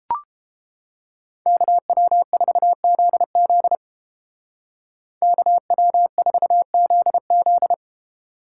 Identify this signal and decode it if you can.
Morse code, heard as a keyed tone, about 31 wpm, 700 Hz. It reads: KW4ZZ KW4ZZ